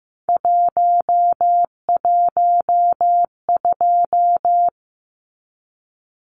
Morse code, heard as a keyed tone, about 15 wpm, 700 Hz. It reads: 112
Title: Numbers - 3-Digit 15wpm